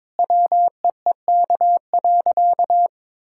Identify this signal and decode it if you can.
Morse code, heard as a keyed tone, about 22 wpm, 700 Hz. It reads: WEEK.
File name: practice / 22wpm / Sentences with 4-Word Phrases - 22wpm